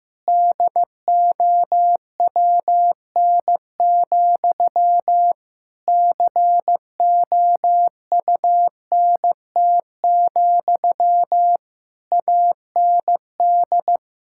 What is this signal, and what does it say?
Morse code, heard as a keyed tone, about 15 wpm, 700 Hz. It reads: DOWN, COUNT, AND
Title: Sentences from Top 1600 Words 15wpm